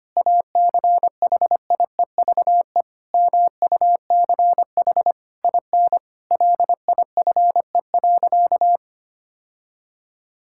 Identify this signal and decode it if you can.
Morse code, heard as a keyed tone, about 25 wpm, 700 Hz. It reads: ACHIEVE MUCH IN LIFE.
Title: Sentences from Top 1000 Words - Review Part 6 - 25wpm